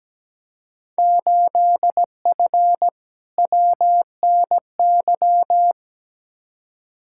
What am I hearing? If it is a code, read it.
Morse code, heard as a keyed tone, about 17 wpm, 700 Hz. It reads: 8F WNY
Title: Field Day Exchanges Encore - 17wpm